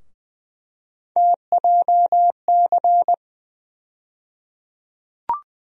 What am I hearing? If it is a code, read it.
Morse code, heard as a keyed tone, about 20 wpm, 700 Hz. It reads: TJC
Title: USA Call Sign Postfixes - 3 Letters 20wpm